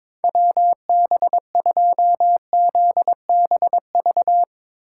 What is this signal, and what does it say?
Morse code, heard as a keyed tone, about 22 wpm, 700 Hz. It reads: WB2ZBV
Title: USA Call Signs - 6 Characters - 3x Character Spacing - 22wpm